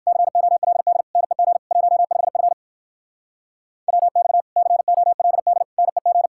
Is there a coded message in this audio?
Yes: Morse code from 0.1 to 6.3 s, heard as a keyed tone, about 59 wpm, 700 Hz. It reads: COLD NEG 15F WX COLD NEG